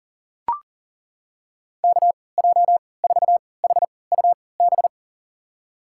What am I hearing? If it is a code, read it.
Morse code, heard as a keyed tone, about 37 wpm, spaced as for 22 wpm, 700 Hz. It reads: KJ4HUB
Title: USA Call Signs - 6 Characters - 3x Character Spacing - 40wpm